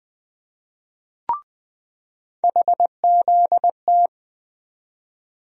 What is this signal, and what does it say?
Morse code, heard as a keyed tone, about 20 wpm, 700 Hz. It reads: HZT
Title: USA Call Sign Postfixes - 3 Letters 20wpm